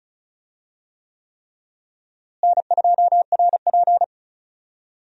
Morse code, heard as a keyed tone, about 36 wpm, 700 Hz. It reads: N2RP
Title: USA Call Signs - 4 Characters - Repeated 3x - 35wpm